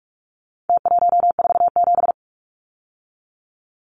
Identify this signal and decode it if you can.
Morse code, heard as a keyed tone, about 44 wpm, 700 Hz. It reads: T147
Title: Numbers - Contest Style - 4-Digit 45wpm